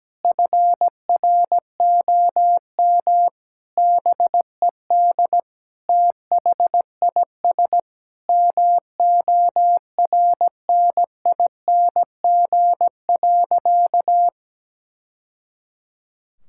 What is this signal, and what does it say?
Morse code, heard as a keyed tone, about 17 wpm, 700 Hz. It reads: FROM BED THIS MORNING.
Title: Sentences from Top 1000 Words - Review Part 6 - 17wpm